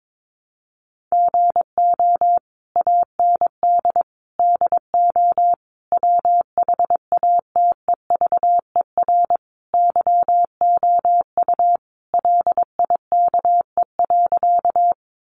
Morse code, heard as a keyed tone, about 22 wpm, 700 Hz. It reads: GO AND DO WHATEVER YOU LIKE.
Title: Sentences from Top 1100 Words 22wpm